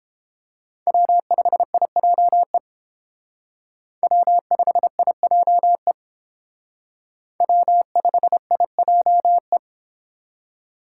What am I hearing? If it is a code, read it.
Morse code, heard as a keyed tone, about 29 wpm, 700 Hz. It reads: W5IJE W5IJE W5IJE